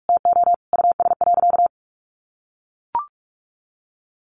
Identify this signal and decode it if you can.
Morse code, heard as a keyed tone, about 44 wpm, 700 Hz. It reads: TO US.
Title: Sentences from Top 600 Words 45wpm